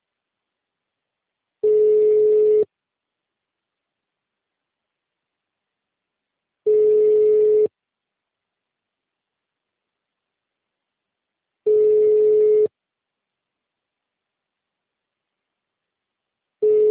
Ringback tone (Europe, 425 Hz).